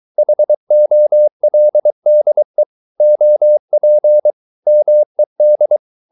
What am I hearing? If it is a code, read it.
Morse code, heard as a keyed tone, about 23 wpm, 600 Hz. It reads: HOLDE OP MED
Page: Ferie 23 wpm